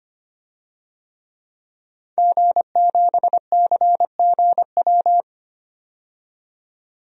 Morse code, heard as a keyed tone, about 25 wpm, 700 Hz. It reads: G7CGW
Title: DXCC-100 Call Signs - 5 Characters - 25wpm